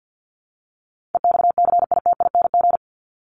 Morse code, heard as a keyed tone, about 50 wpm, 700 Hz. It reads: EXCITING
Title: Top 1751-2000 Words 50wpm